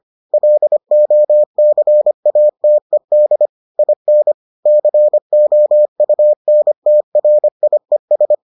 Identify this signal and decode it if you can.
Morse code, heard as a keyed tone, about 25 wpm, 600 Hz. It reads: LOCATED IN COUNTRIES